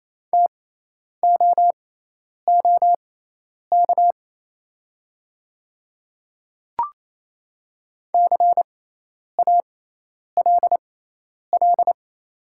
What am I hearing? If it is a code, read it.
Morse code, heard as a keyed tone, about 28 wpm, spaced as for 10 wpm, 700 Hz. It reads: TOOK CALL
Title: 6x Character Spacing - Top 300 Words - 28wpm